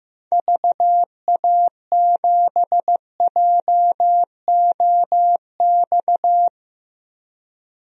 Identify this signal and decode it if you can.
Morse code, heard as a keyed tone, about 16 wpm, 700 Hz. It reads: VA7JOX
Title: DXCC-100 Call Signs - 6 Characters - 15wpm